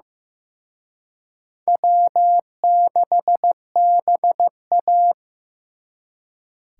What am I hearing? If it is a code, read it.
Morse code, heard as a keyed tone, about 18 wpm, 700 Hz. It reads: W6BA
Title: Speed Racing - USA Call Signs 15wpm